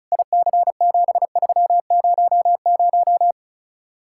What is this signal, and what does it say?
Morse code, heard as a keyed tone, about 43 wpm, 700 Hz. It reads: IC7300
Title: Speed-Racing Common QSO Phrases Encore - 35wpm